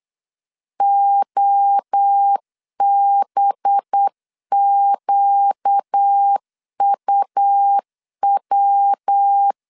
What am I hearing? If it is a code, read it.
Morse code, heard as a keyed tone, about 8 wpm, 800 Hz. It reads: OBQUW
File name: CW / DET / 1 6 1500C S A 8wpm 5203